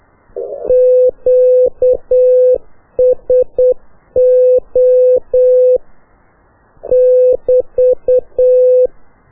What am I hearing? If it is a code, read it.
Morse code, heard as a keyed tone, about 8 wpm, 500 Hz. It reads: QSO <BT>
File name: qso1-qrs